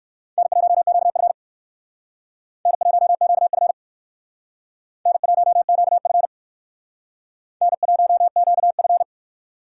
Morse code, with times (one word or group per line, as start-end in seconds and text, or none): 0.4-1.3 N1QF
2.7-3.7 N1QF
5.1-6.3 N1QF
7.6-9.0 N1QF